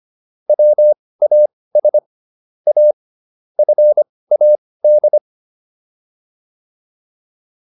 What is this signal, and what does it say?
Morse code, heard as a keyed tone, about 25 wpm, spaced as for 18 wpm, 600 Hz. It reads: WAS A FAD